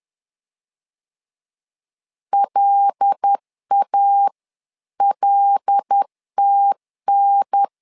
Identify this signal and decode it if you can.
Morse code, heard as a keyed tone, about 10 wpm, 800 Hz. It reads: LALTN
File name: LEZIONI / 5 LBVC / 5 L 10wpm 2259